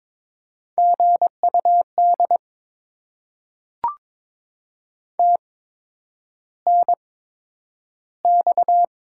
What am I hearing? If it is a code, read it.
Morse code, heard as a keyed tone, about 22 wpm, spaced as for 9 wpm, 700 Hz. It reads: GUD TNX
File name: Top 20 QSO Elements - 8x Character Spacing - 22wpm